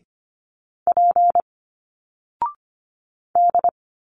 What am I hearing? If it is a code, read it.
Morse code, heard as a keyed tone, about 25 wpm, 700 Hz. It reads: P D
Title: Single Letters 25wpm